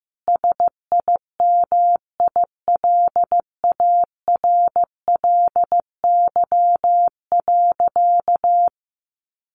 Morse code, heard as a keyed tone, about 15 wpm, 700 Hz. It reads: SIMILARLY.
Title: Sentences from Top 1400 Words 15wpm